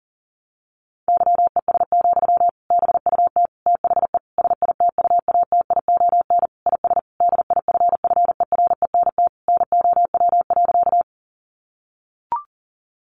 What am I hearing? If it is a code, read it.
Morse code, heard as a keyed tone, about 40 wpm, 700 Hz. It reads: YES, BUT THE SITUATION IS DIFFERENT NOW.
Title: Sentences from Top 1200 Words 40wpm